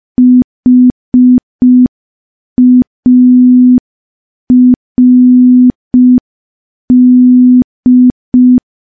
Morse code, heard as a keyed tone, about 5 wpm, 250 Hz. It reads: HARD
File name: morse / morse-training-05WPM